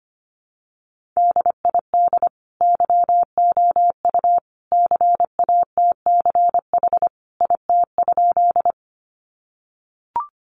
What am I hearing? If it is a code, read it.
Morse code, heard as a keyed tone, about 25 wpm, 700 Hz. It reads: DID YOU CATCH IT?
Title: Sentences from Top 1500 Words 25wpm